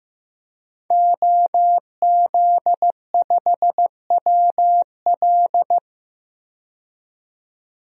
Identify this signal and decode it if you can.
Morse code, heard as a keyed tone, about 15 wpm, 700 Hz. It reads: OZ5WL
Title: DXCC-100 Call Signs - 5 Characters - 15wpm